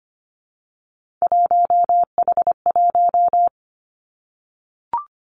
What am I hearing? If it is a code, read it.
Morse code, heard as a keyed tone, about 24 wpm, 700 Hz. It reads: 1H1
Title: Three Challenging Letter-Number Combinations 25wpm